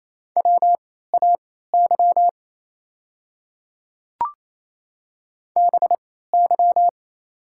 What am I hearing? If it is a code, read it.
Morse code, heard as a keyed tone, about 27 wpm, spaced as for 16 wpm, 690 Hz. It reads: WAY BY
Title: 3x Character Spacing - Top 100 Words - 28wpm